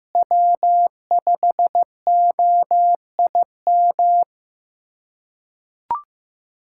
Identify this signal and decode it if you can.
Morse code, heard as a keyed tone, about 15 wpm, 700 Hz. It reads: W5OIM
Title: K1USN SST QSO Practice - 15wpm